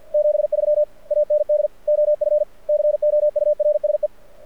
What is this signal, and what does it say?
Morse code, heard as a keyed tone, about 54 wpm, 600 Hz. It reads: 73 AND OF COURSE
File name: archived / CRSnet